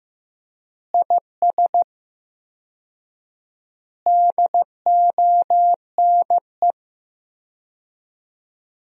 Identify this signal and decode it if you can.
Morse code, heard as a keyed tone, about 15 wpm, 700 Hz. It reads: IS DONE
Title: 4x Word-Spacing - Sentences from Top 500 Words - 15wpm